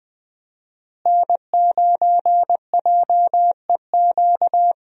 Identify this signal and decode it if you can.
Morse code, heard as a keyed tone, about 20 wpm, 700 Hz. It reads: N9JEQ
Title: USA Call Signs - 5 Characters - Repeated 3x - 20wpm